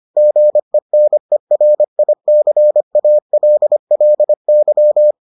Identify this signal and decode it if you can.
Morse code, heard as a keyed tone, about 25 wpm, 600 Hz. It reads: GENERICALLY